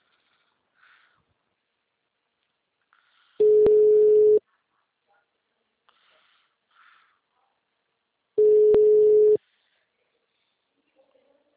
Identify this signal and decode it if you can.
Ringback tone (Europe, 425 Hz).